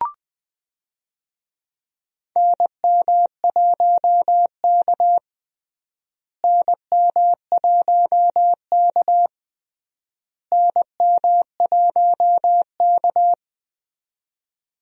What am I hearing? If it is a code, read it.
Morse code, heard as a keyed tone, about 20 wpm, 700 Hz. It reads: NM1K NM1K NM1K
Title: USA Call Signs - 4 Characters - Repeated 3x - 20wpm